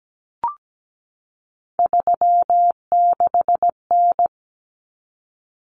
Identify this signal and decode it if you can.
Morse code, heard as a keyed tone, about 17 wpm, 690 Hz. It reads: 36N
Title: Common Cut Numbers - 3-Digit 17wpm